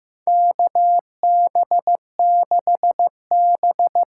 Morse code, heard as a keyed tone, about 15 wpm, 700 Hz. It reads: KB6B
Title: USA Call Signs - 6x Character Spacing - 15wpm